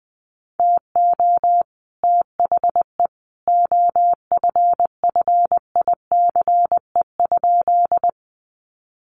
Morse code, heard as a keyed tone, about 20 wpm, 700 Hz. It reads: TO THE OFFICE?
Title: Sentences from Top 1000 Words 20wpm